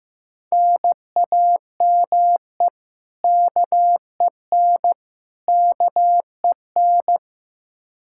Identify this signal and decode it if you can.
Morse code, heard as a keyed tone, about 15 wpm, 700 Hz. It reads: NAME KEN KEN
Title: Practice QSO - Exchange 1 - Beginning - 15wpm